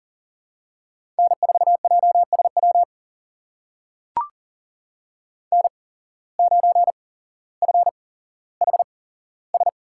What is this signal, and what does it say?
Morse code, heard as a keyed tone, about 40 wpm, spaced as for 17 wpm, 700 Hz. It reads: N4JSW N9FHS
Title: USA Call Signs - 5 Characters - 8x Character Spacing - 40wpm